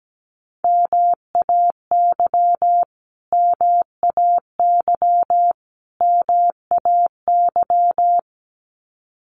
Morse code, heard as a keyed tone, about 17 wpm, 700 Hz. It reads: MAY MAY MAY
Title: Top 500 Words Repeated 17wpm